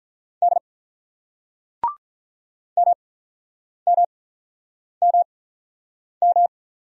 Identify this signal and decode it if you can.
Morse code, heard as a keyed tone, about 45 wpm, spaced as for 9 wpm, 700 Hz. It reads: D MMMM